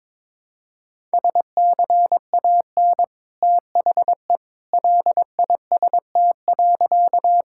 Morse code, heard as a keyed tone, about 22 wpm, 700 Hz. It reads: SCAN THE LIST.
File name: practice / 22wpm / Easy Sentences with Hints - 22wpm